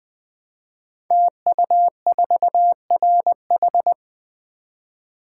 Morse code, heard as a keyed tone, about 20 wpm, 700 Hz. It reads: TU4RH
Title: Code Groups - 5 Letter:Numbers - 20wpm